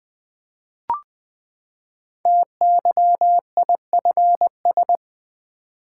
Morse code, heard as a keyed tone, about 20 wpm, 700 Hz. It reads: TYIFS
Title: Code Groups - 5 Letter:Numbers - 20wpm